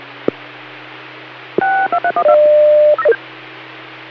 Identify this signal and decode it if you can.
DTMF, heard as telephone touch tones, tones of about 90 ms. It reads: 6231